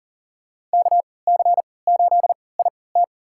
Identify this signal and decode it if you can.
Morse code, heard as a keyed tone, about 40 wpm, spaced as for 23 wpm, 700 Hz. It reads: KC8IT
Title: USA Call Signs - 3x Character Spacing - 40wpm